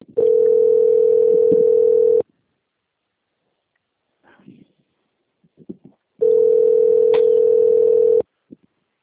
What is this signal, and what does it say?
Ringback tone (North America), 440 + 480 Hz.